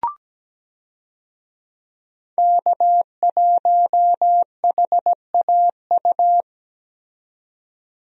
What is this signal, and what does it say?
Morse code, heard as a keyed tone, about 17 wpm, 700 Hz. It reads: K1HAU
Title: USA Call Signs - 5 Characters - 17wpm